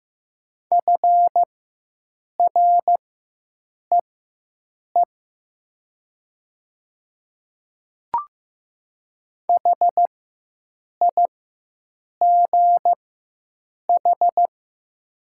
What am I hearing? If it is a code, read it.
Morse code, heard as a keyed tone, about 15 wpm, spaced as for 7 wpm, 690 Hz. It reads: FREE HIGH
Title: 4x Character Spacing - Top 500 Words - 15wpm